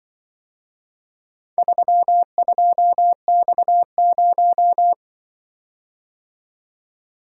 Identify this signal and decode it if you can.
Morse code, heard as a keyed tone, about 23 wpm, 700 Hz. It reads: 32X0
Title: Speed-Racing - Four Letter-Number Combinations - 20wpm